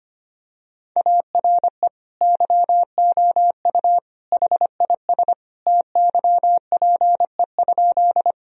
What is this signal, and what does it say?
Morse code, heard as a keyed tone, about 25 wpm, 700 Hz. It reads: ARE YOU HIS TYPE?
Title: Sentences from Top 1000 Words - Review Part 3 - 25wpm